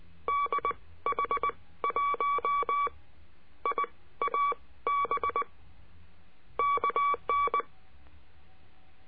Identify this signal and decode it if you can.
Morse code, heard as a keyed tone, about 19 wpm, spaced as for 14 wpm, 1140 Hz. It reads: DH1IAB KN
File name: CURSO DE TELEGRAFIA